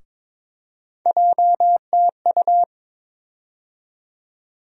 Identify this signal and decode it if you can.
Morse code, heard as a keyed tone, about 22 wpm, 700 Hz. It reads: JTU
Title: USA Call Sign Postfixes - 3 Letters 22wpm